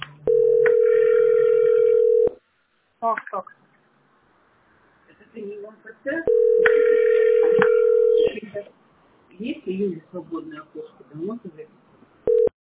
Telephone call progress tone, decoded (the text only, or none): ringback tone